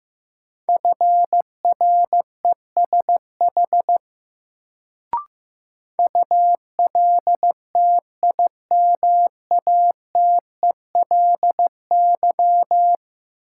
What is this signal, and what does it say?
Morse code, heard as a keyed tone, about 15 wpm, 700 Hz. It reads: FRESH ULTIMATELY
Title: Sentences from Top 1200 Words - New Word Drill - 15wpm